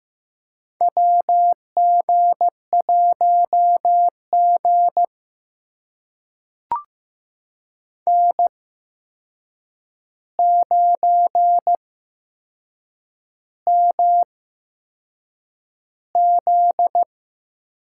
Morse code, heard as a keyed tone, about 15 wpm, spaced as for 6 wpm, 700 Hz. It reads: WG1G N9MZ
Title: USA Call Signs - 4 Characters - 8x Character Spacing - 15wpm